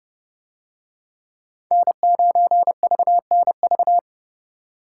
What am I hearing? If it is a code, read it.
Morse code, heard as a keyed tone, about 29 wpm, 700 Hz. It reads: N9VNV